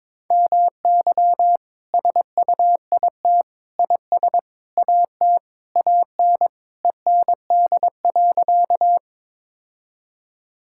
Morse code, heard as a keyed tone, about 22 wpm, 700 Hz. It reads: MY SUIT IS AT AN END.
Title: Sentences from Top 1600 Words 22wpm